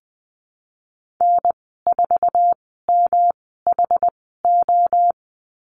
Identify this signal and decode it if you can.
Morse code, heard as a keyed tone, about 20 wpm, spaced as for 14 wpm, 700 Hz. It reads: N4MHO
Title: 161 Lesson 26g - Letter g - Call Signs 20wpm